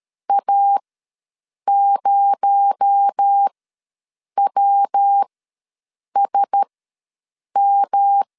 Morse code, heard as a keyed tone, about 13 wpm, spaced as for 7 wpm, 800 Hz. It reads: A0WSM
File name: LEZIONI / 6 PKX0 / 6 PKX0 ws 12wpm 3448